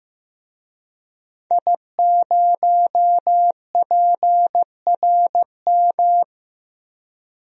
Morse code, heard as a keyed tone, about 15 wpm, 700 Hz. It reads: I0PRM
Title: DXCC-100 Call Signs - 5 Characters - Repeated 3x - 15wpm